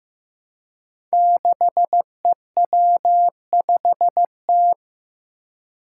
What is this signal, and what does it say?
Morse code, heard as a keyed tone, about 15 wpm, 700 Hz. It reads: 6EW5T
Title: Five Challenging Letter-Number Combinations 15wpm